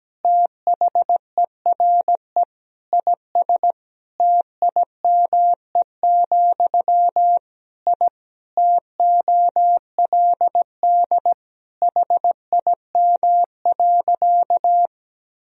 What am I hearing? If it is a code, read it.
Morse code, heard as a keyed tone, about 17 wpm, 700 Hz. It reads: THERE IS TIME, I TOLD HIM.
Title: Sentences from Top 1000 Words - Review Part 1 - 17wpm